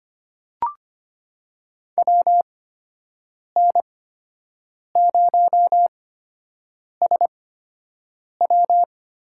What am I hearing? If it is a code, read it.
Morse code, heard as a keyed tone, about 25 wpm, spaced as for 7 wpm, 700 Hz. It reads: WN0SW
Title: USA Call Signs - 5 Characters - 8x Character Spacing - 25wpm